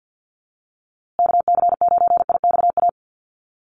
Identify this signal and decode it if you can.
Morse code, heard as a keyed tone, about 50 wpm, 700 Hz. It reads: KC9IXA
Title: USA Call Signs Repeated 3x 50wpm